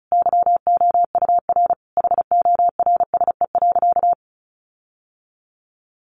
Morse code, heard as a keyed tone, about 34 wpm, 700 Hz. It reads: YOUR HORSE.
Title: Sentences from Top 1500 Words 35wpm